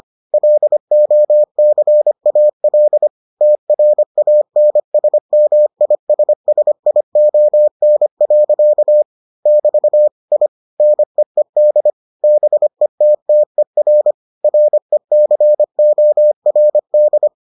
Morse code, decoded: LOCAL TRANSMISSION. <BT> I NEED BETTER RECORD

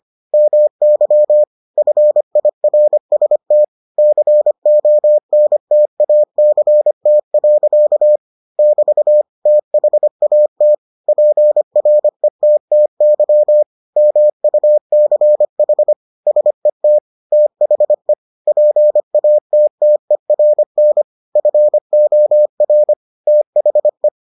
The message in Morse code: MY FIRST CONTACT. <BT> THAT PRETTY MUCH SET THE PATTERN FOR THE

